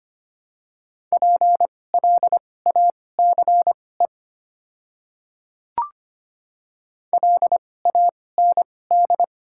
Morse code, heard as a keyed tone, about 25 wpm, spaced as for 18 wpm, 700 Hz. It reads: PLACE LAND
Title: 2x Character Spacing - Top 200 Words - 25wpm